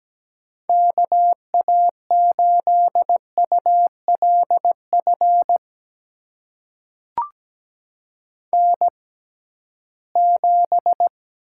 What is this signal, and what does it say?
Morse code, heard as a keyed tone, about 17 wpm, spaced as for 13 wpm, 700 Hz. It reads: KA8ULF N7
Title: USA Call Signs - 6x Character Spacing - 17wpm